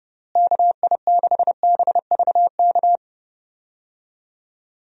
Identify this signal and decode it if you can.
Morse code, heard as a keyed tone, about 30 wpm, 700 Hz. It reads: KI6BVK
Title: USA Call Signs - 6 Characters - Repeated 3x - 30wpm